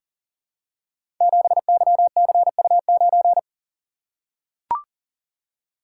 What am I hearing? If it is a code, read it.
Morse code, heard as a keyed tone, about 42 wpm, 700 Hz. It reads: 7YCU9
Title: Code Groups - 5 Letter:Numbers - 40wpm